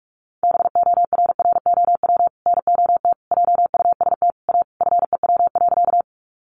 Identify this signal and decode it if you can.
Morse code, heard as a keyed tone, about 44 wpm, 700 Hz. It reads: BORROW NOT JUST A FEW.
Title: Sentences from Top 1500 Words 45wpm